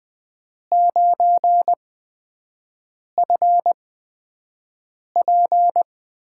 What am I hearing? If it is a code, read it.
Morse code, heard as a keyed tone, about 20 wpm, spaced as for 5 wpm, 700 Hz. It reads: 9FP
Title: USA Call Signs - 8x Character Spacing - 20wpm